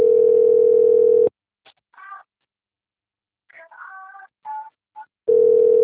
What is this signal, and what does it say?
Ringback tone (North America), 440 + 480 Hz.